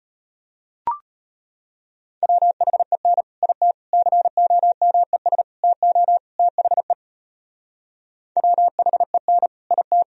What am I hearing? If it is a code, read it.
Morse code, heard as a keyed tone, about 32 wpm, 700 Hz. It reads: WHEN IT COMES TO THE WHEN IT